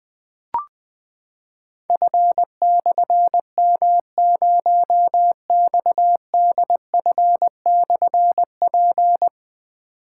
Morse code, heard as a keyed tone, about 20 wpm, 700 Hz. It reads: F/M0XDF/P
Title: Difficult Call Signs 20wpm